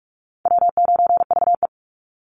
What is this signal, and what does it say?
Morse code, heard as a keyed tone, about 46 wpm, 700 Hz. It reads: W9VE
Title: Call Signs - 45wpm